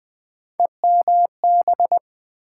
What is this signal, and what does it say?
Morse code, heard as a keyed tone, about 20 wpm, 700 Hz. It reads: EMB
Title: Build A Word - 3 Letters At A Time - 6-Letter Words - 20wpm